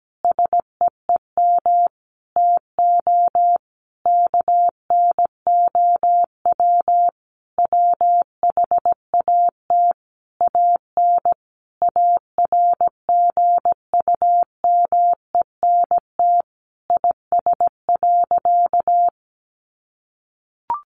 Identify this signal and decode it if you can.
Morse code, heard as a keyed tone, about 17 wpm, 700 Hz. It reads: SEEM TO KNOW WHAT AN ARGUMENT IS.
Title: Sentences from Top 1400 Words 17wpm